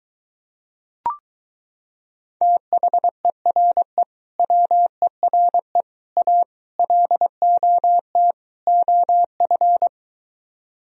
Morse code, heard as a keyed tone, about 23 wpm, 700 Hz. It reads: THERE WERE A LOT OF